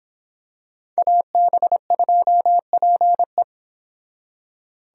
Morse code, heard as a keyed tone, about 26 wpm, 700 Hz. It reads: AB2PE